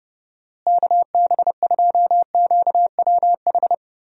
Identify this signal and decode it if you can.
Morse code, heard as a keyed tone, about 30 wpm, 700 Hz. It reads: KB2QWH
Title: USA Call Signs - 6 Characters - Repeated 3x - 30wpm